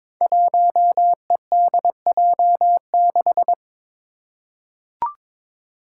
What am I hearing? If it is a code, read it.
Morse code, heard as a keyed tone, about 21 wpm, 700 Hz. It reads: 1EDJ6
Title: Five Challenging Letter-Number Combinations 22wpm